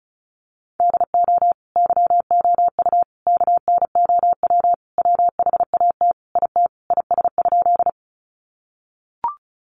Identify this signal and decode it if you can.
Morse code, heard as a keyed tone, about 34 wpm, 700 Hz. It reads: DO YOU KNOW WHAT IT IS?
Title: Sentences from Top 100 Words 35wpm